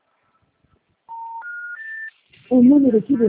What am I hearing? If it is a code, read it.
Special information tone, with speech.